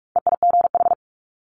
Morse code, heard as a keyed tone, about 46 wpm, 700 Hz. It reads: EIGH
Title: Multi-letter Phonemes 45wpm